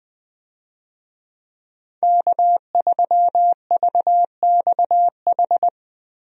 Morse code, heard as a keyed tone, about 20 wpm, 700 Hz. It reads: K3VXH
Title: USA Call Signs - 5 Characters - Repeated 2x - 20wpm